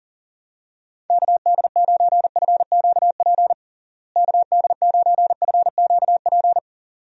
Morse code, heard as a keyed tone, about 40 wpm, 700 Hz. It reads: KD9FQP KD9FQP
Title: USA Call Signs - 6 Characters - Repeated 2x - 40wpm